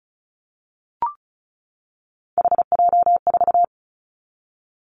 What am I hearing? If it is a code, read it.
Morse code, heard as a keyed tone, about 34 wpm, 700 Hz. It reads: HJ4
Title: Callsign Prefixes for Top 100 DXCC - 35wpm